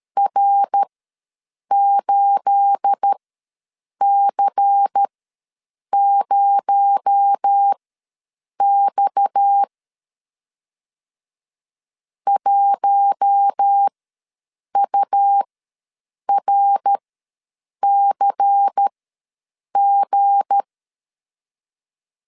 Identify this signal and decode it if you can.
Morse code, heard as a keyed tone, about 13 wpm, spaced as for 7 wpm, 800 Hz. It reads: R8C0X 1URCG
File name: LEZIONI / 9 28 ws 12wpm 3354